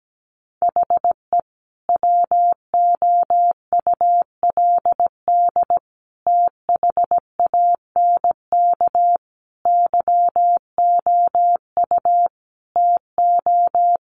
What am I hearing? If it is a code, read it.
Morse code, heard as a keyed tone, about 17 wpm, 700 Hz. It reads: HE WOULD THANK YOU TO
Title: Sentences from Top 1100 Words 17wpm